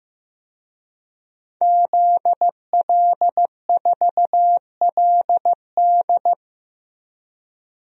Morse code, heard as a keyed tone, about 15 wpm, 700 Hz. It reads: ZL4LD
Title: DXCC-100 Call Signs - 5 Characters - Repeated 3x - 15wpm